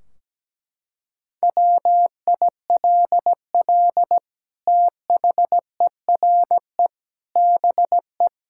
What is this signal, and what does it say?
Morse code, heard as a keyed tone, about 17 wpm, 700 Hz. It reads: WILL THERE BE
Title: Sentences from Top 1600 Words 17wpm